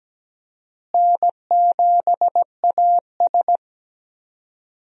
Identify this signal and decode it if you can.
Morse code, heard as a keyed tone, about 17 wpm, 700 Hz. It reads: N7AS
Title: USA Call Signs - 4 Characters - Repeated 3x - 17wpm